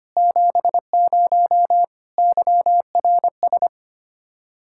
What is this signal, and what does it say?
Morse code, heard as a keyed tone, about 25 wpm, 700 Hz. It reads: 70 YRS
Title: Practice QSO - Exchange 3 - More Info - 25wpm